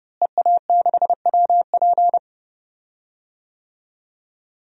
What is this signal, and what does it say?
Morse code, heard as a keyed tone, about 30 wpm, 700 Hz. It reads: EA6WP
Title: DXCC-100 Call Signs - 5 Characters - Repeated 3x - 30wpm